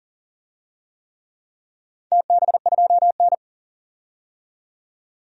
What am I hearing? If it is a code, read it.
Morse code, heard as a keyed tone, about 40 wpm, 700 Hz. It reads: TB2N
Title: DXCC-100 Call Signs - 4 Characters - 40wpm